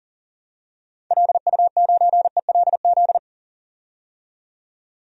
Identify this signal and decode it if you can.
Morse code, heard as a keyed tone, about 39 wpm, 700 Hz. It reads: LU9ELZ
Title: DXCC-100 Call Signs - 6 Characters - 40wpm